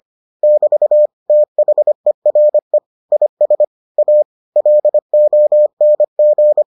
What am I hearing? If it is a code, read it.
Morse code, heard as a keyed tone, about 25 wpm, 600 Hz. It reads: <BT> THERE IS A LONG